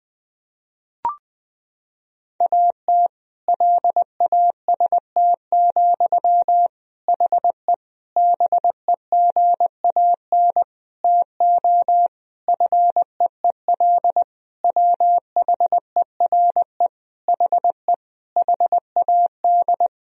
Morse code, heard as a keyed tone, about 20 wpm, 700 Hz. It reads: AT LAST, HE BEGAN TO FEEL WHERE HE HAD
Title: Sentences from Top 1000 Words - Review Part 1 - 20wpm